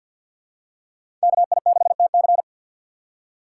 Morse code, heard as a keyed tone, about 47 wpm, 700 Hz. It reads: KI6TC
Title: USA Call Signs - 5 Characters - 3x Character Spacing - 50wpm